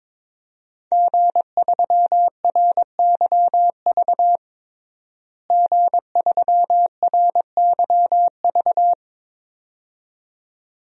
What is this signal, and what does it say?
Morse code, heard as a keyed tone, about 22 wpm, 700 Hz. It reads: G3RYV G3RYV
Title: DXCC-100 Call Signs - 5 Characters - Repeated 2x - 22wpm